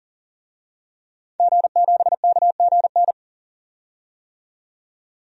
Morse code, heard as a keyed tone, about 40 wpm, 700 Hz. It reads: G7KGN
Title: DXCC-100 Call Signs - 5 Characters - 40wpm